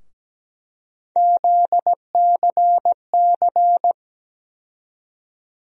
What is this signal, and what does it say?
Morse code, heard as a keyed tone, about 17 wpm, 700 Hz. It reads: ZCC
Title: USA Call Sign Postfixes - 3 Letters 17wpm